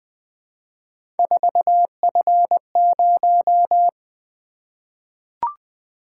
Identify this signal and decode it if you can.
Morse code, heard as a keyed tone, about 20 wpm, 700 Hz. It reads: 4F0
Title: Three Letter-Number Combinations - 20wpm